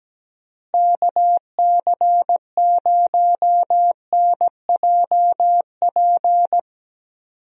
Morse code, heard as a keyed tone, about 17 wpm, 700 Hz. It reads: KC0NJP
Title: USA Call Signs - 6 Characters - Repeated 2x - 17wpm